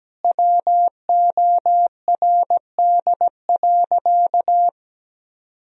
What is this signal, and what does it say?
Morse code, heard as a keyed tone, about 17 wpm, 700 Hz. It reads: WORD.
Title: Sentences from Top 1000 Words - Review Part 2 - 17wpm